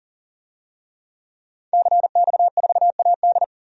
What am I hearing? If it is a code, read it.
Morse code, heard as a keyed tone, about 40 wpm, 700 Hz. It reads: CX4AD
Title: DXCC-100 Call Signs - 5 Characters - 40wpm